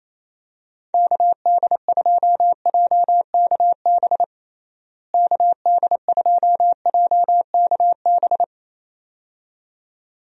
Morse code, heard as a keyed tone, about 27 wpm, 700 Hz. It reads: KD2JKB KD2JKB
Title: USA Call Signs - 6 Characters - Repeated 2x - 28wpm